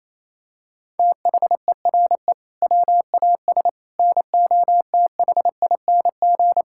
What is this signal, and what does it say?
Morse code, heard as a keyed tone, about 28 wpm, 700 Hz. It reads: THERE WAS NOTHING
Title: Sentences from Top 1000 Words - Review Part 5 - 28wpm